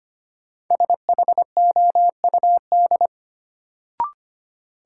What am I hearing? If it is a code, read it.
Morse code, heard as a keyed tone, about 25 wpm, 700 Hz. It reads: SHOUD
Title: Sentences from Top 1000 Words - Review Part 4 - New Word Drill - 25wpm